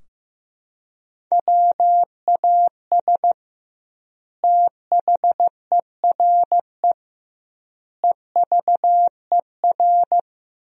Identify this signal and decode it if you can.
Morse code, heard as a keyed tone, about 15 wpm, 700 Hz. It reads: WAS THERE EVER
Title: 2x Word-Spacing - Sentences from Top 400 Words - 15wpm